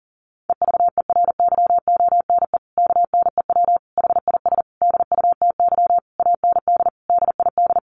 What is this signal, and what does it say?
Morse code, heard as a keyed tone, about 40 wpm, 700 Hz. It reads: EVERYONE KNEW HIS DUTY AND DID
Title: Sentences from Top 1500 Words 40wpm